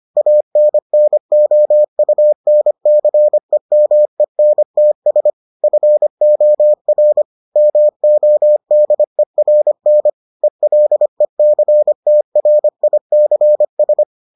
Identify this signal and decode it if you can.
Morse code, heard as a keyed tone, about 25 wpm, 600 Hz. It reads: ANNOUNCEMENTS FOR MODERN ELECTRICS